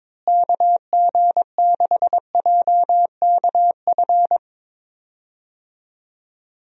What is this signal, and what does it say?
Morse code, heard as a keyed tone, about 21 wpm, 700 Hz. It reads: KG6JKF
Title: USA Call Signs - 6 Characters - Repeated 2x - 22wpm